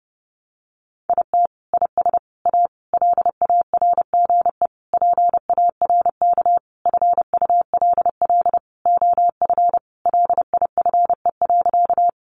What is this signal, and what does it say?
Morse code, heard as a keyed tone, about 30 wpm, 700 Hz. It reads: IT IS A LARGE PARK FULL OF LIFE.
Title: Sentences from Top 1300 Words 30wpm